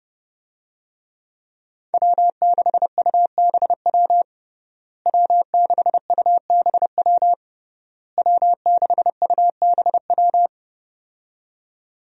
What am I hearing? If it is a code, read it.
Morse code, heard as a keyed tone, about 30 wpm, 700 Hz. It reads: W6UBW W6UBW W6UBW
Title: USA Call Signs - 5 Characters - Repeated 3x - 30wpm